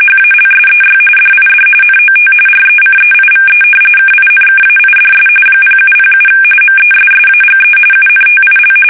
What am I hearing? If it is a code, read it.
RTTY, about 45.45 baud. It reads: HALSEY DID NOT RECEIVE KINKAIDS 0707 MESSAGE UNTIL
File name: fsk850 / battleshipnj